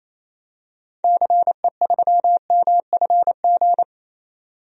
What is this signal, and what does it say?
Morse code, heard as a keyed tone, about 27 wpm, 700 Hz. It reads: CE3MFG
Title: DXCC-100 Call Signs - 6 Characters - 28wpm